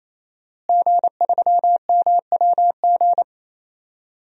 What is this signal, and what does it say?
Morse code, heard as a keyed tone, about 27 wpm, 700 Hz. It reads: G3MWG
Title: DXCC-100 Call Signs - 5 Characters - Repeated 3x - 28wpm